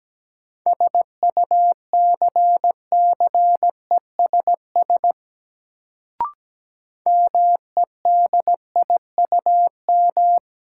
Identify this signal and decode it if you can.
Morse code, heard as a keyed tone, about 17 wpm, 700 Hz. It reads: SUCCESS MEDIUM
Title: Sentences from Top 900 Words - New Word Drill - 17wpm